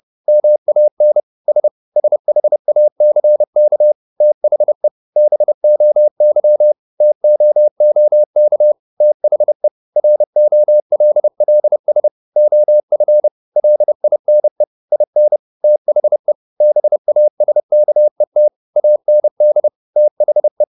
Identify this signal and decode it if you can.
Morse code, heard as a keyed tone, about 30 wpm, 600 Hz. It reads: MAN S SHACK THE BOY TOOK THE ROLLS OF LINE IN THE BASKET AND THE
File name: the old man and the sea / the old man and the sea 1 30wpm